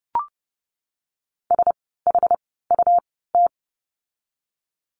Morse code, heard as a keyed tone, about 30 wpm, spaced as for 17 wpm, 700 Hz. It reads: SHUT
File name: Lesson 17f - Letter u - Words 30wpm